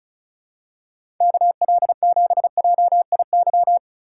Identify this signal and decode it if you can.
Morse code, heard as a keyed tone, about 36 wpm, 700 Hz. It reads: KL7JIY
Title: USA Call Signs - 6 Characters - 6x Character Spacing - 35wpm